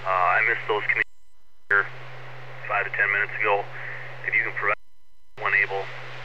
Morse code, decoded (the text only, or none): none